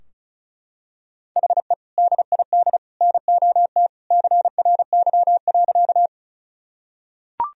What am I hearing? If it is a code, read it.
Morse code, heard as a keyed tone, about 35 wpm, 700 Hz. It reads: HE DID NOT CRY.
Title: Sentences from Top 1000 Words - Review Part 2 - 35wpm